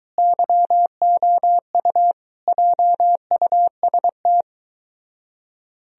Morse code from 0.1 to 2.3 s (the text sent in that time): YOU